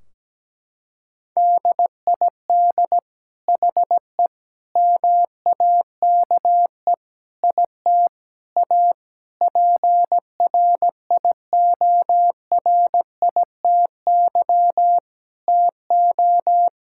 Morse code, heard as a keyed tone, about 17 wpm, 700 Hz. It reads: DID HE MAKE IT A PRIORITY TO
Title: Sentences from Top 1600 Words 17wpm